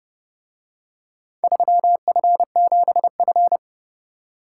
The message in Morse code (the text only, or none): 3F7F